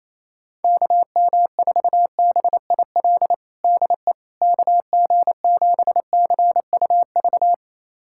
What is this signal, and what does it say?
Morse code, heard as a keyed tone, about 28 wpm, 700 Hz. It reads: KM4BIL DE KG7CUV
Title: USA Callsign DE Callsign - 6 Characters - 28wpm